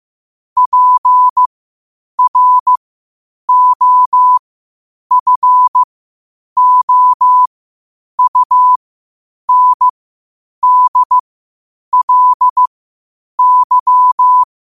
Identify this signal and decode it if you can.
Morse code, heard as a keyed tone, about 15 wpm, spaced as for 8 wpm, 1000 Hz. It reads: PROFOUNDLY
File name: qotdfiles / 8wpm